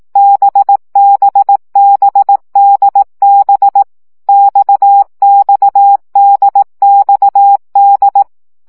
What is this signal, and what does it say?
Morse code, heard as a keyed tone, about 18 wpm, 800 Hz. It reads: BBBDB XXDXD
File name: CW gyakorlas d b x 15 baud